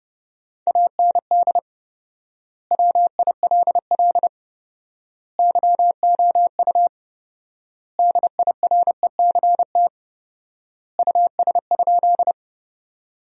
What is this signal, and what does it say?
Morse code, heard as a keyed tone, about 32 wpm, 700 Hz. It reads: AND WILL YOU DIRECT US?
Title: 4x Word-Spacing - Sentences from Top 400 Words - 30wpm